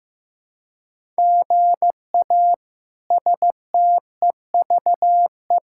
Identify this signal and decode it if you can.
Morse code, heard as a keyed tone, about 16 wpm, 700 Hz. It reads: GA STEVE
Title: K1USN SST QSO Practice - 15wpm